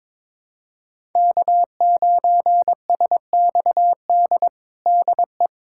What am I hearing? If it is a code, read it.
Morse code, heard as a keyed tone, about 21 wpm, 700 Hz. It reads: K9SXD DE
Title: USA Callsign DE Callsign - 5 Characters - 22wpm